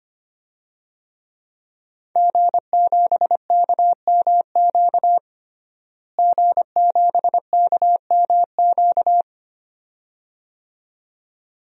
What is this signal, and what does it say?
Morse code, heard as a keyed tone, about 25 wpm, 700 Hz. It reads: G7KMQ G7KMQ
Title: DXCC-100 Call Signs - 5 Characters - Repeated 2x - 25wpm